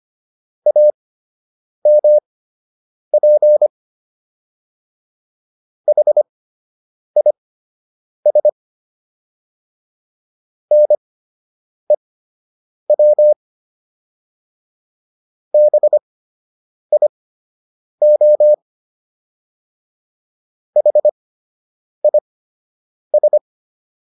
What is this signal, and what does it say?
Morse code, heard as a keyed tone, about 25 wpm, spaced as for 8 wpm, 600 Hz. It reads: AMP HIS NEW BIO HIS